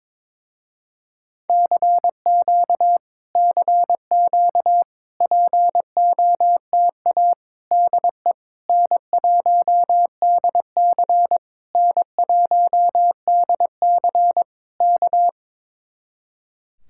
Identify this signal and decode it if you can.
Morse code, heard as a keyed tone, about 22 wpm, 700 Hz. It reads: CQ CQ POTA DE N1DC N1DC K
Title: Speed-Racing Common QSO Phrases - 22wpm